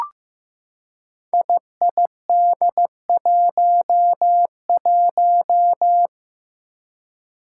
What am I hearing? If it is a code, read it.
Morse code, heard as a keyed tone, about 15 wpm, 700 Hz. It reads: IID11
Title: Five Challenging Letter-Number Combinations 15wpm